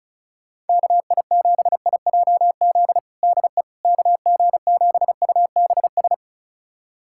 Morse code, heard as a keyed tone, about 36 wpm, 700 Hz. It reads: KI7IJZ DE KG7UBS
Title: USA Callsign DE Callsign 35wpm